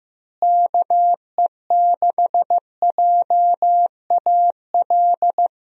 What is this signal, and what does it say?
Morse code, heard as a keyed tone, about 16 wpm, 700 Hz. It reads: KE6JAL
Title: USA Call Signs - 6 Characters - Repeated 2x - 15wpm